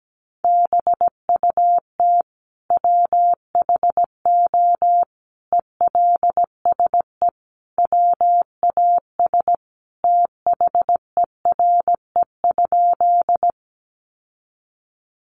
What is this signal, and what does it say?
Morse code, heard as a keyed tone, about 17 wpm, 700 Hz. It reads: BUT WHO ELSE WAS THERE?
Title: Sentences from Top 800 Words 17wpm